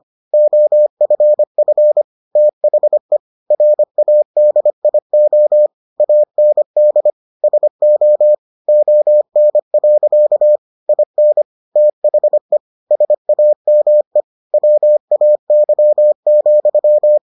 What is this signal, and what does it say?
Morse code, heard as a keyed tone, about 25 wpm, 600 Hz. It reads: OFF THE RADIO AND SO ON. IN THE SAME WAY,